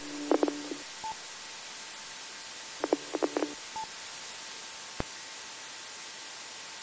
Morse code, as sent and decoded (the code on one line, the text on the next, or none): none